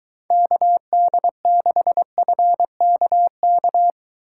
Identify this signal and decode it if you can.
Morse code, heard as a keyed tone, about 23 wpm, 700 Hz. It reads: KD6FKK